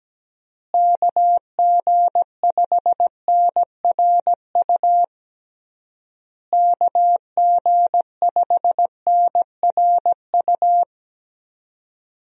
Morse code, heard as a keyed tone, about 17 wpm, 700 Hz. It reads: KG5NRU KG5NRU
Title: USA Call Signs - 6 Characters - Repeated 3x - 17wpm